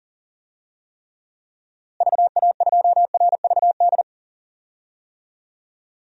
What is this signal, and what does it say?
Morse code, heard as a keyed tone, about 40 wpm, 700 Hz. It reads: VA2RVD
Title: DXCC-100 Call Signs - 6 Characters - Repeated 2x - 40wpm